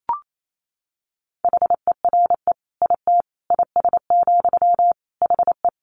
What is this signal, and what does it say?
Morse code, heard as a keyed tone, about 28 wpm, 700 Hz. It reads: HERE IT IS, HE
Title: Sentences from Top 200 Words 28wpm